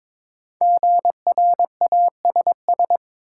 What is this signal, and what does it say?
Morse code, heard as a keyed tone, about 22 wpm, 700 Hz. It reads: GRASS
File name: 5 Letter Words - First 3 Letters Given - 22wpm